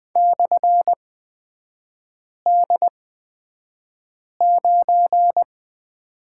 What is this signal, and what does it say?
Morse code, heard as a keyed tone, about 20 wpm, spaced as for 5 wpm, 700 Hz. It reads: /D9
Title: 224 Lesson 36e - BT - Three Characters 20wpm